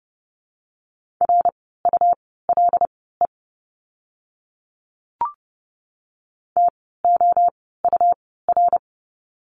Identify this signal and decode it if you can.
Morse code, heard as a keyed tone, about 30 wpm, spaced as for 17 wpm, 700 Hz. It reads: RULE TOUR
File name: Lesson 17f - Letter u - Words 30wpm